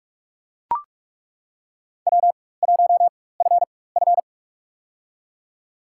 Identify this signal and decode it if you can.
Morse code, heard as a keyed tone, about 42 wpm, spaced as for 21 wpm, 700 Hz. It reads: W1FF
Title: USA Call Signs - 4 Characters - 4x Character Spacing - 45wpm